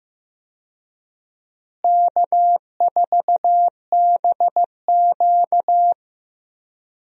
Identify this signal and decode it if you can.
Morse code, heard as a keyed tone, about 16 wpm, 700 Hz. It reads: K4BQ
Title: USA Call Signs - 4 Characters - Repeated 2x - 15wpm